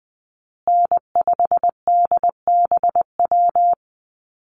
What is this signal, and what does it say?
Morse code, heard as a keyed tone, about 20 wpm, 700 Hz. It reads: N5DBW
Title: USA Call Signs Repeated 2x 20wpm